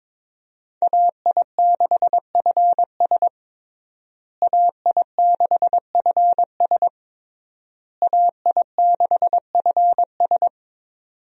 Morse code, heard as a keyed tone, about 22 wpm, 700 Hz. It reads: AI6FS AI6FS AI6FS
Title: USA Call Signs - 5 Characters - Repeated 3x - 22wpm